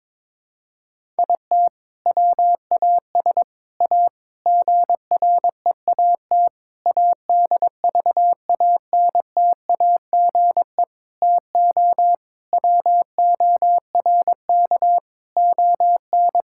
Morse code, heard as a keyed tone, about 22 wpm, 700 Hz. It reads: IT WAS A GREAT ADVANTAGE TO WORK ON
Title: Sentences from Top 1000 Words - Review Part 6 - 22wpm